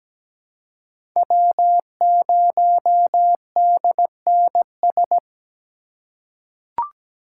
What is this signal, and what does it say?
Morse code, heard as a keyed tone, about 17 wpm, 700 Hz. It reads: W0DNS
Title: USA Call Signs - 5 Characters - 3x Character Spacing - 17wpm